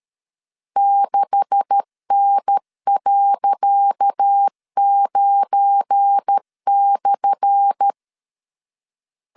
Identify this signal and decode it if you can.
Morse code, heard as a keyed tone, about 13 wpm, 800 Hz. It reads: 6N.9/
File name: LEZIONI / 10 SIGNS / 10D COMA 12wpm 2544